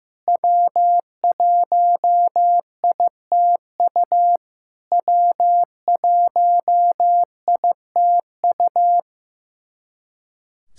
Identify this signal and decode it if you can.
Morse code, heard as a keyed tone, about 16 wpm, 700 Hz. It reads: W1ITU W1ITU
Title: SOTA QSO Practice 15wpm